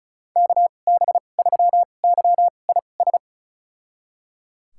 Morse code, heard as a keyed tone, about 34 wpm, spaced as for 25 wpm, 700 Hz. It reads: KB3YIS
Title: USA Call Signs - 6 Characters - 2x Character Spacing - 35wpm